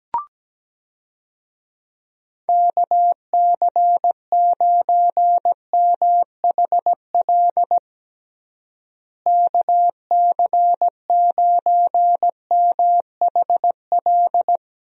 Morse code, heard as a keyed tone, about 17 wpm, 700 Hz. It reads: KC9MHL KC9MHL
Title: USA Call Signs - 6 Characters - Repeated 2x - 17wpm